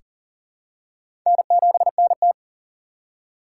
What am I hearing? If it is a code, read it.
Morse code, heard as a keyed tone, about 40 wpm, 700 Hz. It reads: N7NT
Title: K1USN SST QSO Practice - 40wpm